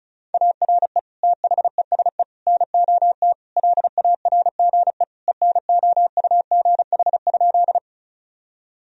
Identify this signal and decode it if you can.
Morse code, heard as a keyed tone, about 36 wpm, 700 Hz. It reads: ARE THESE NOT LARGE ENOUGH?
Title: Sentences from Top 1000 Words - Review Part 1 - 35wpm